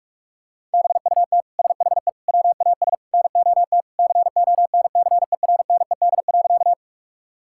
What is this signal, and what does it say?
Morse code, heard as a keyed tone, about 41 wpm, 700 Hz. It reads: BUT SHE WAS NOT CONCERNED.
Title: Sentences from Top 1000 Words - Review Part 6 - 45wpm